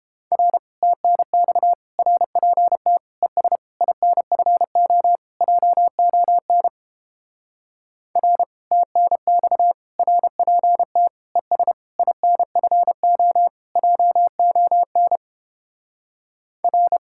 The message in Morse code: R TNX RPT ES INFO JON R TNX RPT ES INFO JON R

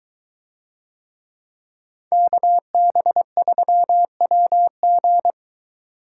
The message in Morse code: KB3WG

